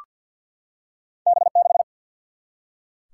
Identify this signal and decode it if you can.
Morse code, heard as a keyed tone, about 53 wpm, 700 Hz. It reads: B 6
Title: Single-Letter Single-Number 50wpm